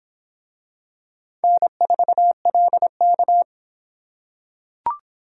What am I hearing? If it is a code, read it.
Morse code, heard as a keyed tone, about 23 wpm, 700 Hz. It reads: N4LK